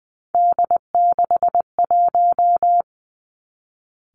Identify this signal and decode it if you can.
Morse code, heard as a keyed tone, about 20 wpm, 700 Hz. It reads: D61
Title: Three Challenging Letter-Number Combinations 20wpm